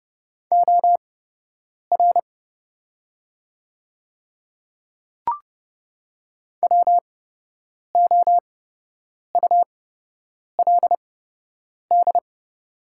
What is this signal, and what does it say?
Morse code, heard as a keyed tone, about 30 wpm, spaced as for 8 wpm, 700 Hz. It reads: OR WOULD
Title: 8x Character Spacing - Top 100 Words - 30wpm